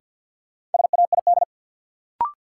Morse code, heard as a keyed tone, about 53 wpm, 700 Hz. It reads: SAID
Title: Sets of 2 Words - New Word Drill - 50wpm